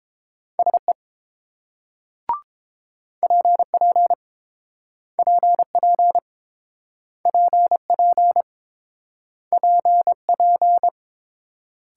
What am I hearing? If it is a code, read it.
Morse code, heard as a keyed tone, about 30 wpm, 700 Hz. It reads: SE PP PP PP PP